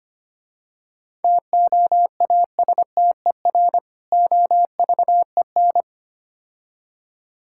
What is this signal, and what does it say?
Morse code, heard as a keyed tone, about 28 wpm, 700 Hz. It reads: TOASTER OVEN
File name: practice / 25wpm / Difficult Words - Speed-Racing with Hints - 25wpm